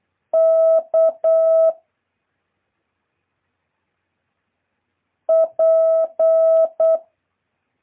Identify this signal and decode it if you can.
Morse code, heard as a keyed tone, about 8 wpm, 650 Hz. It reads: K P
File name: jh8cbh / 2023cw